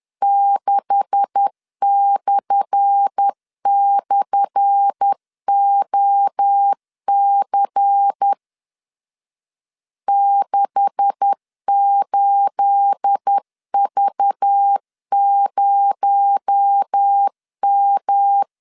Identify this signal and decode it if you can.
Morse code, heard as a keyed tone, about 10 wpm, 800 Hz. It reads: 6//OC 68V0M
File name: LEZIONI / 10 SIGNS / 10A SLASH 10wpm 2808